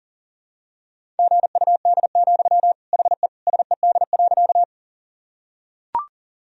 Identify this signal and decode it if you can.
Morse code, heard as a keyed tone, about 40 wpm, 700 Hz. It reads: GUD, HE SED.
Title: Sentences from Top 200 Words and Abbreviations 40wpm